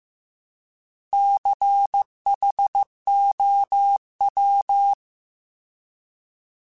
Morse code, heard as a keyed tone, about 15 wpm, 800 Hz. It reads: CHOW